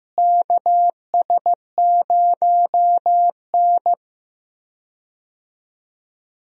Morse code, heard as a keyed tone, about 16 wpm, 700 Hz. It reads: KS0N
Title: USA Call Signs - 4 Characters - Repeated 2x - 15wpm